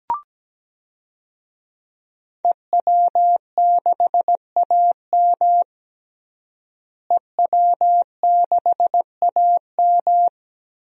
Morse code, heard as a keyed tone, about 17 wpm, 700 Hz. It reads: EW6AM EW6AM
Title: DXCC-100 Call Signs - 5 Characters - Repeated 3x - 17wpm